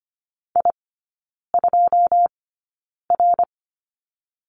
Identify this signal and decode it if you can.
Morse code, heard as a keyed tone, about 25 wpm, spaced as for 9 wpm, 690 Hz. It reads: I2R
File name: Lesson 14e - Number 2 - Three Characters 25wpm